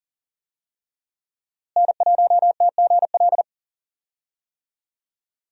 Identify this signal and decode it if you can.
Morse code, heard as a keyed tone, about 40 wpm, 700 Hz. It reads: N1TGL
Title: USA Call Signs - 5 Characters - Repeated 2x - 40wpm